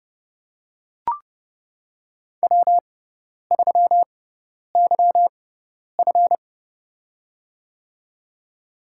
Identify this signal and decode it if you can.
Morse code, heard as a keyed tone, about 32 wpm, spaced as for 10 wpm, 700 Hz. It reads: W3YF
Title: USA Call Signs - 4 Characters - 6x Character Spacing - 30wpm